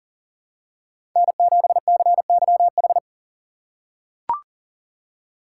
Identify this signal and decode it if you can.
Morse code, heard as a keyed tone, about 40 wpm, 700 Hz. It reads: N7CYH
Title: USA Call Signs - 4x Character Spacing - 40wpm